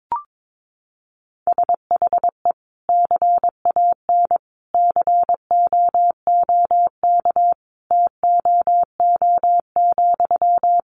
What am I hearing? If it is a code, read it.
Morse code, heard as a keyed tone, about 22 wpm, 700 Hz. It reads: SHE CAN COOK TOO,
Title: Sentences from Top 700 Words 22wpm